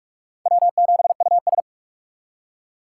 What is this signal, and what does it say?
Morse code, heard as a keyed tone, about 46 wpm, 700 Hz. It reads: W7US
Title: USA Call Signs - 4 Characters - 6x Character Spacing - 45wpm